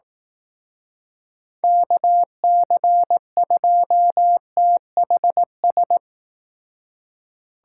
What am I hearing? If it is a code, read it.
Morse code, heard as a keyed tone, about 18 wpm, 700 Hz. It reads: KC2THS
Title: Speed Racing - USA Call Signs 15wpm